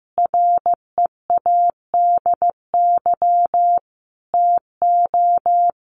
Morse code, heard as a keyed tone, about 15 wpm, 700 Hz. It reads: READY TO
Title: Sentences from Top 700 Words 15wpm